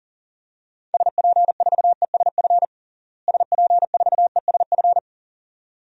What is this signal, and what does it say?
Morse code, heard as a keyed tone, about 40 wpm, 700 Hz. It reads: SP4ESF SP4ESF
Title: DXCC-100 Call Signs - 6 Characters - Repeated 2x - 40wpm